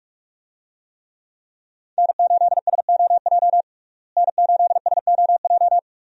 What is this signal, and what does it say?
Morse code, heard as a keyed tone, about 47 wpm, 700 Hz. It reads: N8SOJ N8SOJ
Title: USA Call Signs - 5 Characters - Repeated 2x - 45wpm